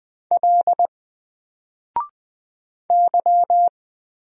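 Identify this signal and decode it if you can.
Morse code, heard as a keyed tone, about 20 wpm, 700 Hz. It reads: L Y
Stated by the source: Single letter and number Morse drill at 20 wpm, from N4LCD.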